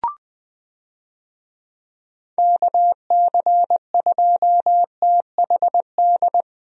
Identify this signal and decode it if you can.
Morse code, heard as a keyed tone, about 20 wpm, 700 Hz. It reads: KC2THD
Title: USA Call Signs - 6 Characters - Repeated 3x - 20wpm